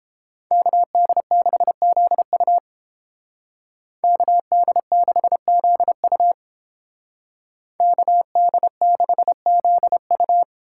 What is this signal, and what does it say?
Morse code, heard as a keyed tone, about 29 wpm, 700 Hz. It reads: KD6ZU KD6ZU KD6ZU